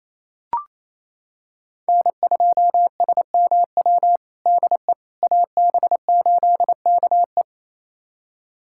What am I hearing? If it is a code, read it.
Morse code, heard as a keyed tone, about 28 wpm, 700 Hz. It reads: N2SMW DE AB8KE
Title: USA Callsign DE Callsign - 5 Characters - 28wpm